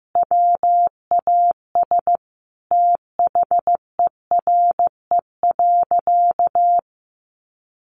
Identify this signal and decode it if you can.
Morse code, heard as a keyed tone, about 15 wpm, 700 Hz. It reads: WAS THERE.
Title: Sentences from Top 1400 Words 15wpm